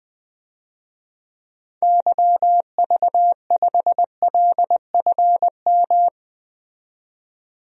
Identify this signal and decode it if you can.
Morse code, heard as a keyed tone, about 21 wpm, 700 Hz. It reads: YV5LFM
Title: DXCC-100 Call Signs - 6 Characters - 20wpm